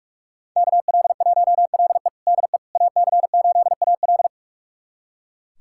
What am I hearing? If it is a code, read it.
Morse code, heard as a keyed tone, about 41 wpm, 700 Hz. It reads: KL1LE DE AC8AL
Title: USA Callsign DE Callsign - 5 Characters - 45wpm